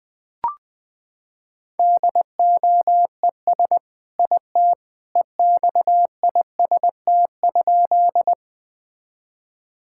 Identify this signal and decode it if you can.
Morse code, heard as a keyed tone, about 20 wpm, 700 Hz. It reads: DOES IT EXIST?
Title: Sentences from Top 1000 Words - Review Part 5 - 20wpm